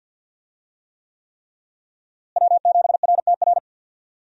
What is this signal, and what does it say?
Morse code, heard as a keyed tone, about 46 wpm, 700 Hz. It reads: W7RTR
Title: USA Call Signs - 5 Characters - Repeated 3x - 50wpm